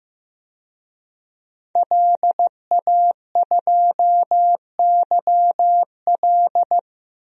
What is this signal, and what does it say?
Morse code, heard as a keyed tone, about 15 wpm, 700 Hz. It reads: LA2YL
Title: DXCC-100 Call Signs - 5 Characters - Repeated 2x - 15wpm